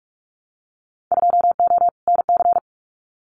Morse code, heard as a keyed tone, about 44 wpm, 700 Hz. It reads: 2O NC
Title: Winter Field Day Exchanges 45wpm